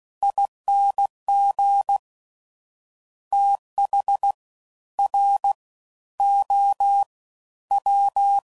Morse code, heard as a keyed tone, about 16 wpm, spaced as for 11 wpm, 800 Hz. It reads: ING THROW
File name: Prefixes / FLs & THs / Fast TH